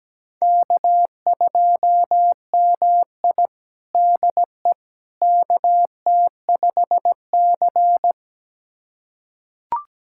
Morse code, heard as a keyed tone, about 17 wpm, 700 Hz. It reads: K2MI DE KT5C
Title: USA Callsign DE Callsign - 4 Characters - 17wpm